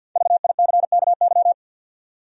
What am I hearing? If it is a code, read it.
Morse code, heard as a keyed tone, about 48 wpm, 700 Hz. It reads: VICKY
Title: USA Common Names - 50wpm